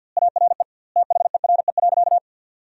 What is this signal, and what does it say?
Morse code, heard as a keyed tone, about 50 wpm, 700 Hz. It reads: ARE THERE.
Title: 2x Word-Spacing - Sentences from Top 500 Words - 50wpm